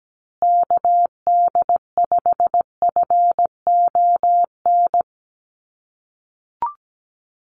Morse code, heard as a keyed tone, about 17 wpm, 700 Hz. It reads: KD5FON
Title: USA Call Signs Repeated 3x 17wpm